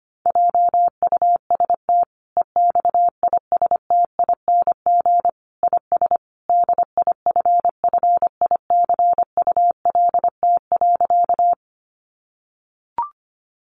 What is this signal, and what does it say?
Morse code, heard as a keyed tone, about 25 wpm, 700 Hz. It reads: JUST EXISTING IS DIFFICULT.
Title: Sentences from Top 1500 Words 25wpm